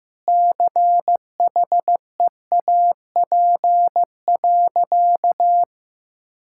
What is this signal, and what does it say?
Morse code, heard as a keyed tone, about 15 wpm, 700 Hz. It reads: CHEAP.
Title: Sentences from Top 1600 Words 15wpm